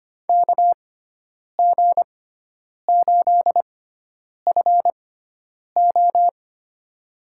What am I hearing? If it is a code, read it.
Morse code, heard as a keyed tone, about 25 wpm, spaced as for 9 wpm, 700 Hz. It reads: KG8FO
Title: USA Call Signs - 5 Characters - 6x Character Spacing - 25wpm